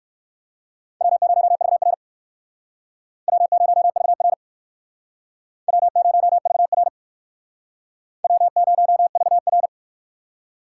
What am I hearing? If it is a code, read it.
Morse code, heard as a keyed tone, about 56 wpm, 700 Hz. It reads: W0VR W0VR W0VR W0VR